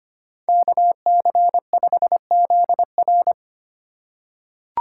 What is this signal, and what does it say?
Morse code, heard as a keyed tone, about 25 wpm, 700 Hz. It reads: KC5ZR
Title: USA Call Signs - 2x Character Spacing - 25wpm